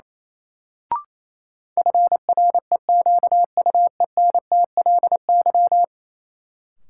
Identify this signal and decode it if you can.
Morse code, heard as a keyed tone, about 28 wpm, 700 Hz. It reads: FREQUENTLY
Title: Sentences from Top 1300 Words - New Word Drill Alt - 28wpm